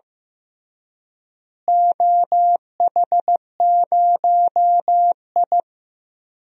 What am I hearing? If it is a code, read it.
Morse code, heard as a keyed tone, about 15 wpm, 700 Hz. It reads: OH0I
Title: DXCC-100 Call Signs - 4 Characters - Repeated 3x - 15wpm